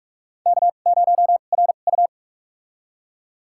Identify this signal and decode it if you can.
Morse code, heard as a keyed tone, about 46 wpm, spaced as for 33 wpm, 700 Hz. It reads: K0RU
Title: USA Call Signs - 4 Characters - 2x Character Spacing - 45wpm